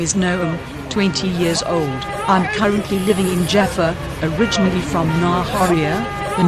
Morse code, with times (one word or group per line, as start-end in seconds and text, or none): none